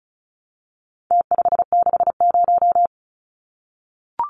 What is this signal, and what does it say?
Morse code, heard as a keyed tone, about 34 wpm, 700 Hz. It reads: T560
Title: Numbers - Contest Style - 4-Digit 35wpm